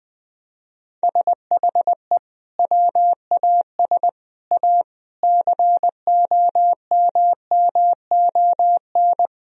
Morse code, heard as a keyed tone, about 21 wpm, 700 Hz. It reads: SHE WAS A COMMON
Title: Sentences from Top 1000 Words - Review Part 4 - 20wpm